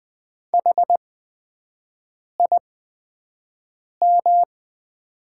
Morse code, heard as a keyed tone, about 21 wpm, spaced as for 6 wpm, 700 Hz. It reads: HIM
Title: 8x Character Spacing - Top 100 Words - 20wpm